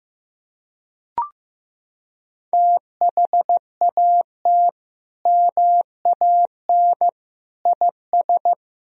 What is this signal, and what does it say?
Morse code, heard as a keyed tone, about 15 wpm, 700 Hz. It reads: THAT MAN IS
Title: Sentences from Top 1000 Words - Review Part 1 - 15wpm